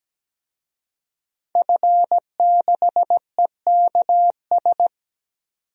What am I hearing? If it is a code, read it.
Morse code, heard as a keyed tone, about 17 wpm, 700 Hz. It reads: F6EKS
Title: DXCC-100 Call Signs - 5 Characters - 17wpm